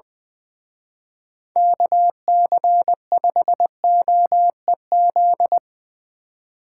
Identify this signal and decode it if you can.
Morse code, heard as a keyed tone, about 22 wpm, 700 Hz. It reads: KC5OEZ
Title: Speed Racing - USA 2x3 Call Signs 20wpm